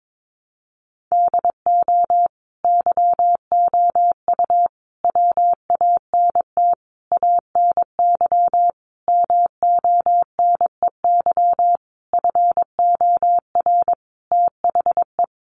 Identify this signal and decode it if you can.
Morse code, heard as a keyed tone, about 22 wpm, 700 Hz. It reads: DO YOU WANT ANY MONEY FOR THE
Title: Sentences from Top 1400 Words 22wpm